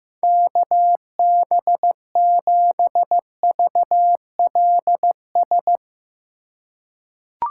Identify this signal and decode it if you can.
Morse code, heard as a keyed tone, about 16 wpm, 700 Hz. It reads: KB7VLS
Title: USA Call Signs - 6 Characters - 2x Character Spacing - 15wpm